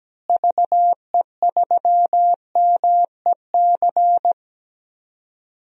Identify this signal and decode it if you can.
Morse code, heard as a keyed tone, about 17 wpm, 700 Hz. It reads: VE3MEC
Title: DXCC-100 Call Signs - 6 Characters - Repeated 3x - 17wpm